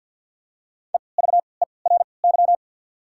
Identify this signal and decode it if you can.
Morse code, heard as a keyed tone, about 53 wpm, spaced as for 29 wpm, 700 Hz. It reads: EVERY
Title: 3x Character Spacing - Top 200 Words - 50wpm